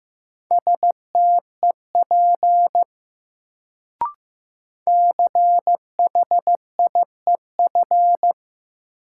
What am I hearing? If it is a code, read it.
Morse code, heard as a keyed tone, about 15 wpm, 700 Hz. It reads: STEP CHIEF
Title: Sets of 5 Words Encore - New Word Drill - 15wpm